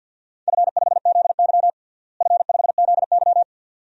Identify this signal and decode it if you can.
Morse code, heard as a keyed tone, about 53 wpm, 700 Hz. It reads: F5ZY F5ZY
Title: DXCC-100 Call Signs - 4 Characters - Repeated 2x - 50wpm